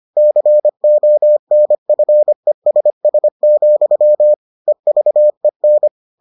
Morse code, heard as a keyed tone, about 25 wpm, 600 Hz. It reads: CONFESS, EVEN